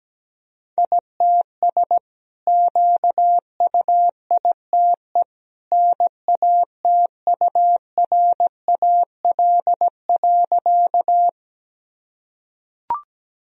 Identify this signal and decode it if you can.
Morse code, heard as a keyed tone, about 17 wpm, 700 Hz. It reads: ITS QUITE NATURAL.
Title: Sentences from Top 1000 Words - Review Part 4 - 17wpm